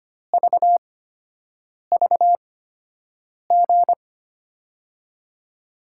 Morse code, heard as a keyed tone, about 25 wpm, spaced as for 7 wpm, 700 Hz. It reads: VVG
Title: USA Call Signs - 6 Characters - 8x Character Spacing - 25wpm